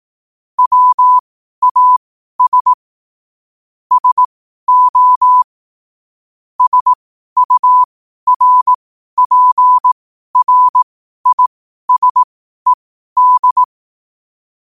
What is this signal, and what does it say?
Morse code, heard as a keyed tone, about 18 wpm, spaced as for 12 wpm, 1000 Hz. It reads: WAS SO SURPRISED
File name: qotdfiles / 12wpm